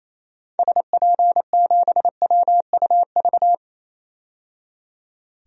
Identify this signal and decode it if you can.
Morse code, heard as a keyed tone, about 28 wpm, 700 Hz. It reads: SP7WUV
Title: DXCC-100 Call Signs - 6 Characters - 28wpm